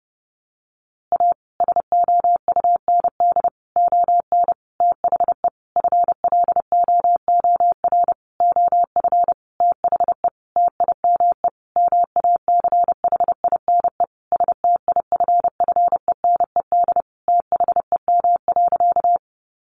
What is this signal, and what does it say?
Morse code, heard as a keyed tone, about 28 wpm, 700 Hz. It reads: A SOUND ON THE FLOOR OF THE TIME MACHINE STIFFENED THEM.
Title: A Sound Of Thunder 30wpm